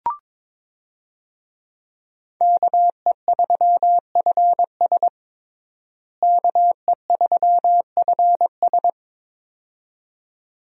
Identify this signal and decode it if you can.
Morse code, heard as a keyed tone, about 22 wpm, 700 Hz. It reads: KE3FS KE3FS
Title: USA Call Signs - 5 Characters - Repeated 2x - 22wpm